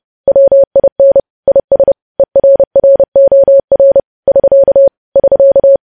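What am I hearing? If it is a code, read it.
Morse code, heard as a keyed tone, about 30 wpm, 550 Hz. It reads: WIN IS ERROR <SK> <SK>
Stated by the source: Morse news headlines sent at 30 wpm, opening with vvv.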